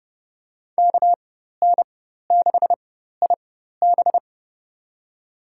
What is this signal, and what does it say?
Morse code, heard as a keyed tone, about 30 wpm, spaced as for 14 wpm, 700 Hz. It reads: KN6IB
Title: USA Call Signs - 5 Characters - 4x Character Spacing - 30wpm